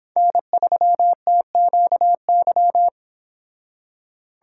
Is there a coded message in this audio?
Yes: Morse code from 0.2 to 2.9 s, heard as a keyed tone, about 26 wpm, 700 Hz. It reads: N3TQY